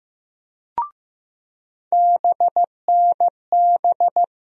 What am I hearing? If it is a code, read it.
Morse code, heard as a keyed tone, about 15 wpm, 700 Hz. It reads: BNB
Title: USA Call Sign Postfixes - 3 Letters 15wpm